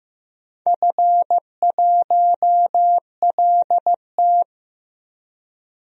Morse code, heard as a keyed tone, about 15 wpm, 700 Hz. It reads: F1LT
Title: DXCC-100 Call Signs - 4 Characters - 15wpm